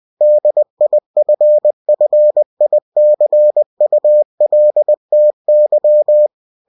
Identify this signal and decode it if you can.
Morse code, heard as a keyed tone, about 20 wpm, 600 Hz. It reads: DIFFICULTY